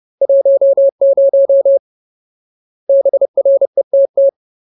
Morse code, heard as a keyed tone, about 30 wpm, 550 Hz. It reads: 10 BRETT
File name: Ultimate Word List / 30WPM Ultimate